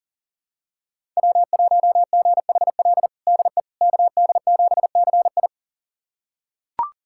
Morse code, heard as a keyed tone, about 40 wpm, 700 Hz. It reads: W1GHL DE KD7CI
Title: USA Callsign DE Callsign - 5 Characters - 40wpm